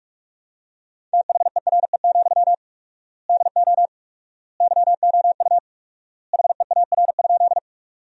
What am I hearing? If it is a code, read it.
Morse code, heard as a keyed tone, about 47 wpm, 700 Hz. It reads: THERE, DO YOU HEAR?
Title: 4x Word-Spacing - Sentences from Top 300 Words - 45wpm